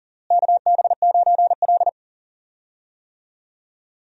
Morse code, heard as a keyed tone, about 40 wpm, 700 Hz. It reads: KB9L
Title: USA Call Signs - 4 Characters - Repeated 2x - 40wpm